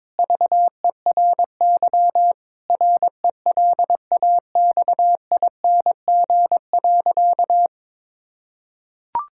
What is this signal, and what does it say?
Morse code, heard as a keyed tone, about 22 wpm, 700 Hz. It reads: VERY RELAXING.
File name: Difficult Sentences with Hints - 22wpm